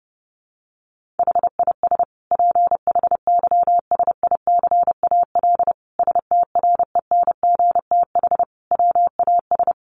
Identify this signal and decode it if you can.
Morse code, heard as a keyed tone, about 30 wpm, 700 Hz. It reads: HIS PHYSICAL STRENGTH WAS
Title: Sentences from Top 1400 Words 30wpm